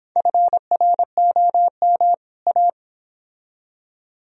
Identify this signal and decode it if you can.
Morse code, heard as a keyed tone, about 29 wpm, 700 Hz. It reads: FROM A